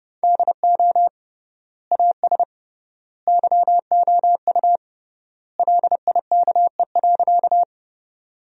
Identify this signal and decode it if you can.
Morse code, heard as a keyed tone, about 30 wpm, 700 Hz. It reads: DO AS YOU LIKE.
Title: 3x Word-Spacing - Sentences from Top 100 Words - 30wpm